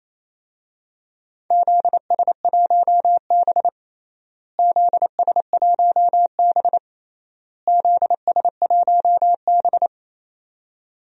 Morse code, heard as a keyed tone, about 28 wpm, 700 Hz. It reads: ZS1B ZS1B ZS1B
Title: DXCC-100 Call Signs - 4 Characters - Repeated 3x - 28wpm